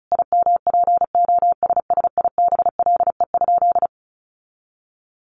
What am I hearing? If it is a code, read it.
Morse code, heard as a keyed tone, about 34 wpm, 700 Hz. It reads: IMPOSSIBLE?
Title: Sentences from Top 1100 Words 35wpm